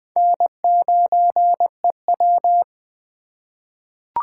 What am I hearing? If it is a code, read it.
Morse code, heard as a keyed tone, about 20 wpm, 700 Hz. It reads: N9EW
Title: USA Call Signs - 4 Characters - 2x Character Spacing - 20wpm